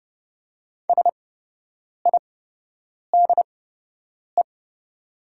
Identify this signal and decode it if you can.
Morse code, heard as a keyed tone, about 29 wpm, spaced as for 8 wpm, 700 Hz. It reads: SIDE
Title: 8x Character Spacing - Top 300 Words - 30wpm